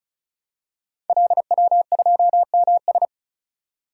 Morse code, heard as a keyed tone, about 36 wpm, 700 Hz. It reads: LW2MS
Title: Code Groups - 5 Letter:Numbers - 35wpm